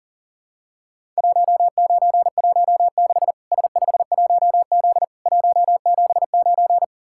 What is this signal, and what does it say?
Morse code, heard as a keyed tone, about 40 wpm, 700 Hz. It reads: 1916 S51Z 179
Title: Worked All Europe DX Contest - QTC Exchange - 40wpm